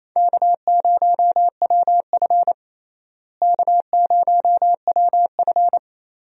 Morse code, heard as a keyed tone, about 28 wpm, 700 Hz. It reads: K0WF K0WF
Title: USA Call Signs - 4 Characters - Repeated 3x - 28wpm